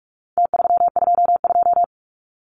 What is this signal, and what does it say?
Morse code, heard as a keyed tone, about 44 wpm, 700 Hz. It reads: T322
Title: Numbers - Contest Style - 4-Digit 45wpm